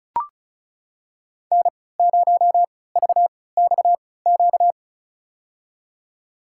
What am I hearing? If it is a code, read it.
Morse code, heard as a keyed tone, about 34 wpm, spaced as for 20 wpm, 700 Hz. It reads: N0VXQ
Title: USA Call Signs - 5 Characters - 3x Character Spacing - 35wpm